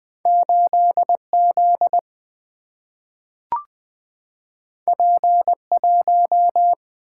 Morse code, heard as a keyed tone, about 20 wpm, 700 Hz. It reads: 8Z P1
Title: Single-Letter Single-Number 20wpm